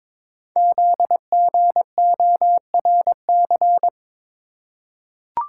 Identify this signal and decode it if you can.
Morse code, heard as a keyed tone, about 22 wpm, 700 Hz. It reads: ZGORC
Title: Code Groups - 5 Letters - 22wpm